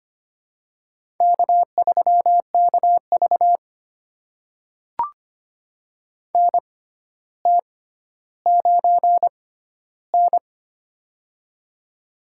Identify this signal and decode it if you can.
Morse code, heard as a keyed tone, about 25 wpm, spaced as for 13 wpm, 700 Hz. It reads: K3KV NT9N
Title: USA Call Signs - 4 Characters - 6x Character Spacing - 25wpm